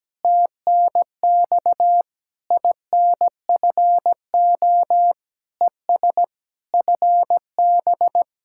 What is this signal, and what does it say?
Morse code, heard as a keyed tone, about 17 wpm, 700 Hz. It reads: TNX INFO ES FB
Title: Speed-Racing Common QSO Phrases - 17wpm